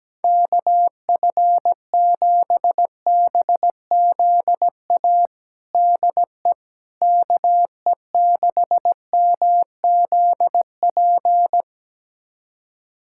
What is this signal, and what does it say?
Morse code, heard as a keyed tone, about 17 wpm, 700 Hz. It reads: KF7BZA DE KE6MZP
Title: USA Callsign DE Callsign - 6 Characters - 17wpm